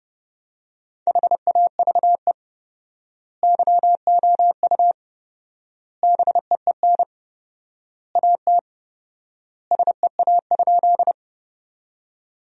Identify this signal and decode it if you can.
Morse code, heard as a keyed tone, about 30 wpm, 700 Hz. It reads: HAVE YOU BEEN AT SEA?
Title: 4x Word-Spacing - Sentences from Top 300 Words - 30wpm